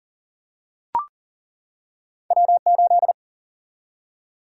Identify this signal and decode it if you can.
Morse code, heard as a keyed tone, about 40 wpm, 700 Hz. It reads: W8
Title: Single-Letter Single-Number 40wpm